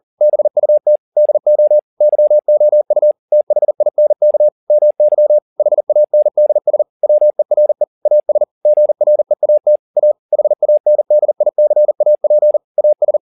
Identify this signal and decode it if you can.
Morse code, heard as a keyed tone, about 40 wpm, 600 Hz. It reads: BUT DO YOU THINK MY HANDS WERE AS GREAT A HANDICAP AS
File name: the old man and the sea 2 40wpm